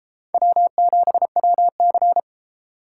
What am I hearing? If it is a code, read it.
Morse code, heard as a keyed tone, about 31 wpm, 700 Hz. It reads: W7WC